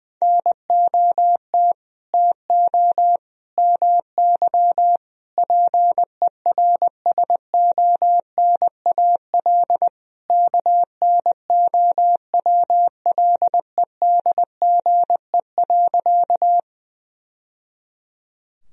Morse code, heard as a keyed tone, about 21 wpm, 700 Hz. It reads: NOT TO MY PERSONAL KNOWLEDGE.
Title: Sentences from Top 1000 Words - Review Part 4 - 20wpm